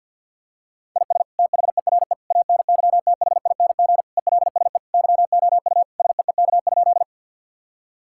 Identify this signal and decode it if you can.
Morse code, heard as a keyed tone, about 51 wpm, 700 Hz. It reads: IS THERE ANYTHING ELSE YOU SEEK?
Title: Sentences from Top 1600 Words 50wpm